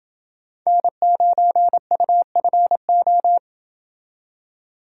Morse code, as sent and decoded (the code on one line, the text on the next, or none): -. ----. ..- ..-. ---
N9UFO